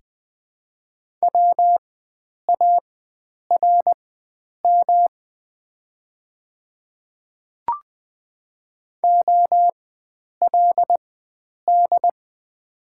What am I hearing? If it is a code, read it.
Morse code, heard as a keyed tone, about 20 wpm, spaced as for 9 wpm, 700 Hz. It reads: WARM OLD
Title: 4x Character Spacing - Top 500 Words - 20wpm